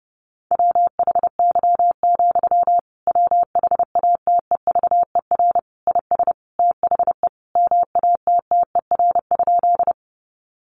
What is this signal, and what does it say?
Morse code, heard as a keyed tone, about 30 wpm, 700 Hz. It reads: WHY, WHATEVER IS THE MATTER?
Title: Sentences from Top 1100 Words 30wpm